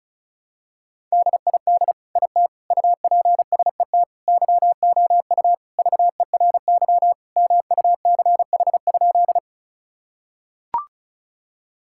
Morse code, heard as a keyed tone, about 36 wpm, 700 Hz. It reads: DID IT UPSET YOU VERY MUCH?
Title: Sentences from Top 1000 Words - Review Part 5 - 35wpm